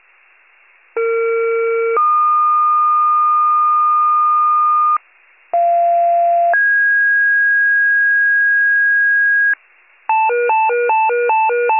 Two-tone page, heard as a one-second tone, then a three-second tone, about 460 then 1220 Hz, about 690 then 1650 Hz, with noise. It is heard from 1.0 to 9.5 s.
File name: pager / 41 RESCUE DUTY / wcalert